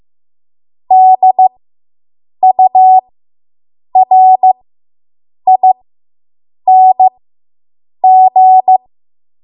Morse code, heard as a keyed tone, about 15 wpm, spaced as for 7 wpm, 750 Hz. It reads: DURING